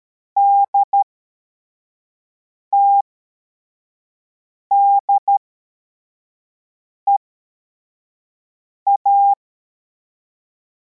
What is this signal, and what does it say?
Morse code, heard as a keyed tone, about 13 wpm, spaced as for 4 wpm, 800 Hz. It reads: DTDEA
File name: formation / cw / Text24